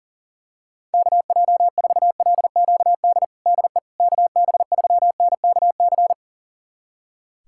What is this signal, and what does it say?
Morse code, heard as a keyed tone, about 40 wpm, 700 Hz. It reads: KJ4LQD DE KB3NKC
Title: USA Callsign DE Callsign - 6 Characters - 40wpm